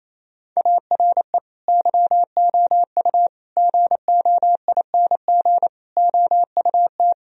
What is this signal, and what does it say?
Morse code, heard as a keyed tone, about 28 wpm, 700 Hz. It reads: ARE YOU GOING OUT
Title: Sentences from Top 1000 Words - Review Part 2 - 28wpm